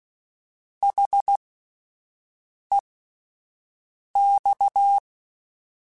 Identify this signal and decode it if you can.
Morse code, heard as a keyed tone, about 16 wpm, spaced as for 5 wpm, 800 Hz. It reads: HEX